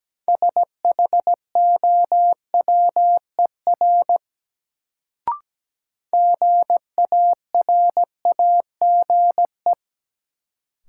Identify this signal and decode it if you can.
Morse code, heard as a keyed tone, about 17 wpm, 700 Hz. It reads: SHOWER GARAGE
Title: Sentences from Top 1600 Words - New Word Drill Alt - 17wpm